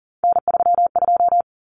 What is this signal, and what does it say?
Morse code, heard as a keyed tone, about 40 wpm, 700 Hz. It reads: N32
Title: Common Cut Numbers - 3-Digit 40wpm